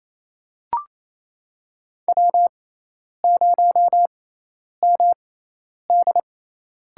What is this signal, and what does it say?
Morse code, heard as a keyed tone, about 28 wpm, spaced as for 10 wpm, 700 Hz. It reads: W0MD
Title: USA Call Signs - 5 Characters - 6x Character Spacing - 28wpm